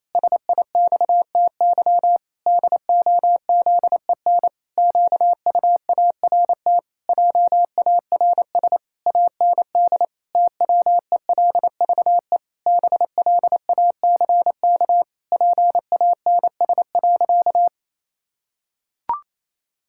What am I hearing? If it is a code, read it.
Morse code, heard as a keyed tone, about 28 wpm, 700 Hz. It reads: SIXTY DOZEN QUART JARS AND TWELVE BLACK PANS.
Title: Pangram Challenge v2 - 28wpm